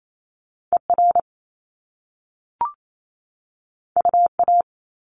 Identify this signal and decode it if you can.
Morse code, heard as a keyed tone, about 28 wpm, 700 Hz. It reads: ER UA
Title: Top Two Letter Combinations in Words 28wpm